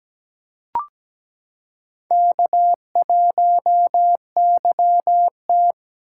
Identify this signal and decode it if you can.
Morse code, heard as a keyed tone, about 17 wpm, 700 Hz. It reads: K1YT
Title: K1USN SST QSO Practice - 17wpm